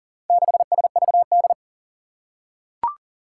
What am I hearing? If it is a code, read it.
Morse code, heard as a keyed tone, about 40 wpm, 700 Hz. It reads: 6SVD
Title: Four Challenging Letter-Number Combinations 40wpm